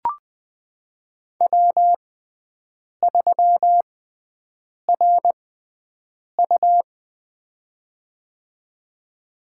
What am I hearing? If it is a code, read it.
Morse code, heard as a keyed tone, about 20 wpm, spaced as for 7 wpm, 700 Hz. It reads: W3RU
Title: USA Call Signs - 4 Characters - 6x Character Spacing - 20wpm